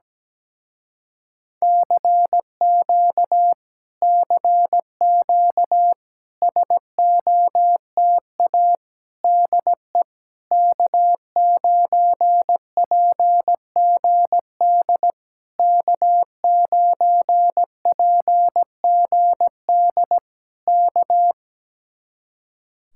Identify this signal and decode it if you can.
Morse code, heard as a keyed tone, about 17 wpm, 700 Hz. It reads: CQ CQ SOTA DE K9PGD K9PGD K
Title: Speed-Racing Common QSO Phrases - 17wpm